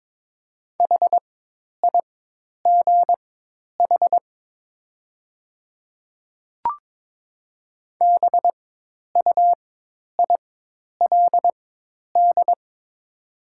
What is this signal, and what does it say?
Morse code, heard as a keyed tone, about 22 wpm, spaced as for 10 wpm, 700 Hz. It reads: HIGH BUILD
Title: 4x Character Spacing - Top 300 Words - 22wpm